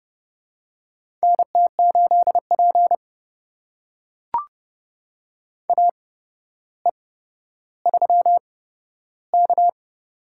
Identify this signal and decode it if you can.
Morse code, heard as a keyed tone, about 30 wpm, spaced as for 13 wpm, 700 Hz. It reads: NT8P AE3K
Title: USA Call Signs - 4 Characters - 8x Character Spacing - 30wpm